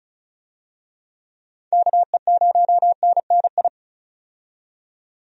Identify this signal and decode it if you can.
Morse code, heard as a keyed tone, about 36 wpm, 700 Hz. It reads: KE0NNI
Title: USA Call Signs - 6 Characters - 35wpm